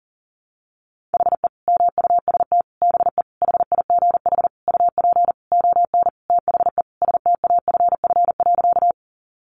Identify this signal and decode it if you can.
Morse code, heard as a keyed tone, about 40 wpm, 700 Hz. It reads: HE MUST BE HIGH UP ON THE STAFF.
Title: Sentences from Top 1200 Words 40wpm